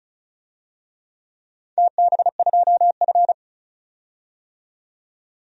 Morse code, heard as a keyed tone, about 36 wpm, 700 Hz. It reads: TB2F
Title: DXCC-100 Call Signs - 4 Characters - 35wpm